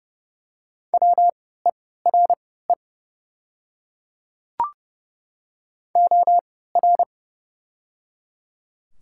Morse code, heard as a keyed tone, about 30 wpm, spaced as for 17 wpm, 700 Hz. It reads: WERE OR
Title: 3x Character Spacing - Top 100 Words - 30wpm